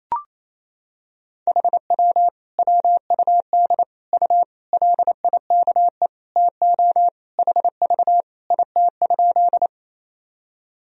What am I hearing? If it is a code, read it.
Morse code, heard as a keyed tone, about 28 wpm, 700 Hz. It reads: HW WUD U LIKE TO HV IT?
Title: Sentences from Top 100 Words and Abbreviations 28wpm